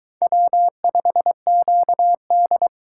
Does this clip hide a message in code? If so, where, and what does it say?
Yes: Morse code from 0.2 to 2.7 s, heard as a keyed tone, about 21 wpm, 700 Hz. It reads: W5QD